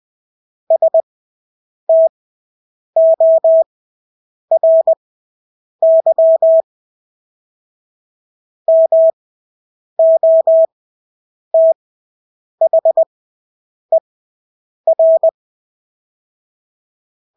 Morse code, heard as a keyed tone, about 20 wpm, spaced as for 8 wpm, 650 Hz. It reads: STORY MOTHER
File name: Engelske ord 20/8 wpm